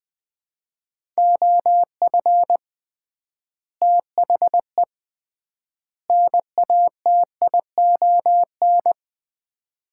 Morse code, heard as a keyed tone, about 21 wpm, 700 Hz. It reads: OF THE NATION
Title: 3x Word-Spacing - Sets of 3 Words - 20wpm